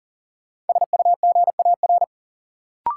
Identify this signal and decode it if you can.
Morse code, heard as a keyed tone, about 40 wpm, 700 Hz. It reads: SUGAR
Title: Sentences from Top 1200 Words - New Word Drill - 40wpm